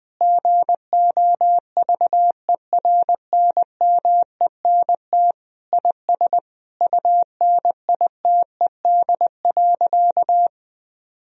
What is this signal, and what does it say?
Morse code, heard as a keyed tone, about 20 wpm, 700 Hz. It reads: GOVERNMENT IS UNITED.
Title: Sentences from Top 1000 Words - Review Part 4 - 20wpm